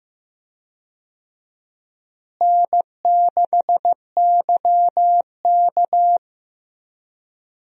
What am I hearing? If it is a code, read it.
Morse code, heard as a keyed tone, about 15 wpm, 700 Hz. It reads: N6YK
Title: USA Call Signs - 4 Characters - Repeated 3x - 15wpm